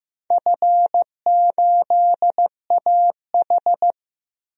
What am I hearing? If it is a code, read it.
Morse code, heard as a keyed tone, about 15 wpm, 700 Hz. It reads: F8AH
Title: DXCC-100 Call Signs - 4 Characters - 15wpm